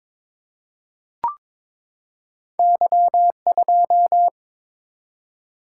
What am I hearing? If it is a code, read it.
Morse code, heard as a keyed tone, about 22 wpm, 700 Hz. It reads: Y2
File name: practice / 22wpm / Two Letter-Number Combinations - 22wpm